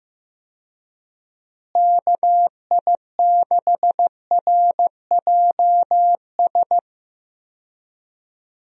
Morse code, heard as a keyed tone, about 15 wpm, 700 Hz. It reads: KI6RJS
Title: USA Call Signs - 6 Characters - Repeated 2x - 15wpm